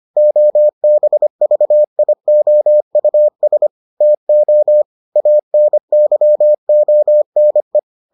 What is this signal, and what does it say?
Morse code, heard as a keyed tone, about 25 wpm, 600 Hz. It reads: OBVIOUS TO ANYONE